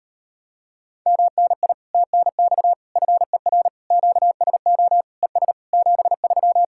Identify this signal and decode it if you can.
Morse code, heard as a keyed tone, about 33 wpm, 700 Hz. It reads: MNI TNX FER QSO ES 73